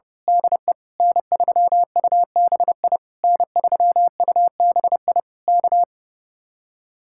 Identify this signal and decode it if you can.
Morse code, heard as a keyed tone, about 28 wpm, 700 Hz. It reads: DE N3UBI N3UBI K
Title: Speed-Racing Common QSO Phrases - 30wpm